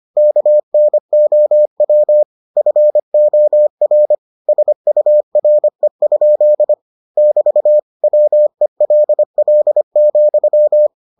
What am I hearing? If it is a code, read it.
Morse code, heard as a keyed tone, about 25 wpm, 600 Hz. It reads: KNOW FOR SURE? <BT> WELL,